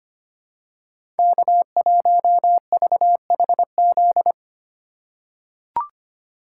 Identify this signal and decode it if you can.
Morse code, heard as a keyed tone, about 25 wpm, 700 Hz. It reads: K1VHZ
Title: USA Call Signs - 2x Character Spacing - 25wpm